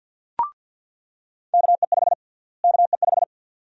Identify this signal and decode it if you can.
Morse code, heard as a keyed tone, about 48 wpm, 700 Hz. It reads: KE5 KE5
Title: USA Call Sign Prefixes - Repeated 2x - 50wpm